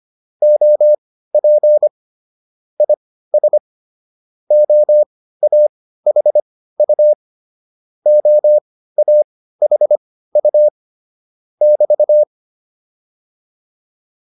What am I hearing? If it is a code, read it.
Morse code, heard as a keyed tone, about 25 wpm, spaced as for 15 wpm, 600 Hz. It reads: OP IS OAHU OAHU <BT>
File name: QSO104 15